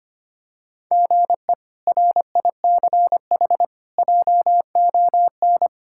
Morse code, heard as a keyed tone, about 25 wpm, 700 Hz. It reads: GE RICH JON
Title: K1USN SST QSO Practice - 25wpm